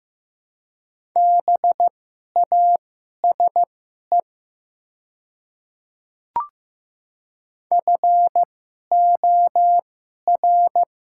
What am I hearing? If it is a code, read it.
Morse code, heard as a keyed tone, about 15 wpm, spaced as for 11 wpm, 700 Hz. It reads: BASE FOR
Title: 2x Character Spacing - Top 300 Words - 15wpm